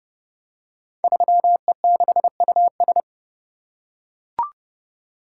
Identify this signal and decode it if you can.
Morse code, heard as a keyed tone, about 30 wpm, 700 Hz. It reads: 3E6US
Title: Code Groups - 5 Letter:Numbers - 30wpm